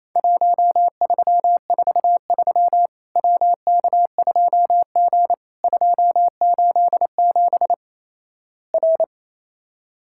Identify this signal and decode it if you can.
Morse code, heard as a keyed tone, about 27 wpm, 700 Hz. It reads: 1343 WK2G 287 R
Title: Worked All Europe DX Contest - QTC Exchange - 28wpm